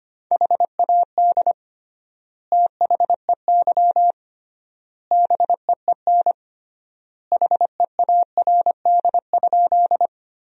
Morse code, heard as a keyed tone, about 25 wpm, 700 Hz. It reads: HAD THEY BEEN HEARD?
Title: 3x Word-Spacing - Sentences from Top 400 Words - 25wpm